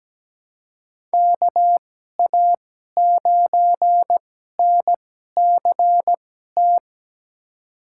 Morse code, heard as a keyed tone, about 17 wpm, spaced as for 12 wpm, 700 Hz. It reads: KA9NCT
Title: USA Call Signs - 2x Character Spacing - 17wpm